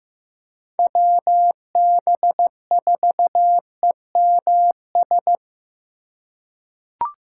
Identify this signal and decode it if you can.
Morse code, heard as a keyed tone, about 15 wpm, 700 Hz. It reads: WB4EMS
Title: USA Call Signs - 6x Character Spacing - 15wpm